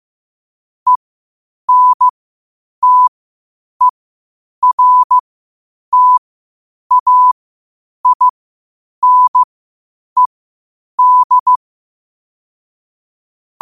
Morse code, heard as a keyed tone, about 15 wpm, spaced as for 8 wpm, 1000 Hz. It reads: ENTERTAINED
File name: qotdfiles / 8wpm